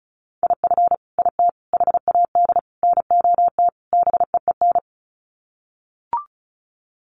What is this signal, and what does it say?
Morse code, heard as a keyed tone, about 34 wpm, 700 Hz. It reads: IF IT HAD NOT BEEN
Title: Sets of 5 Words Encore 35wpm